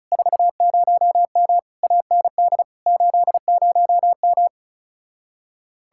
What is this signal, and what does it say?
Morse code, heard as a keyed tone, about 37 wpm, 700 Hz. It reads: 40M AND 80M
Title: Speed-Racing Common QSO Phrases - 30wpm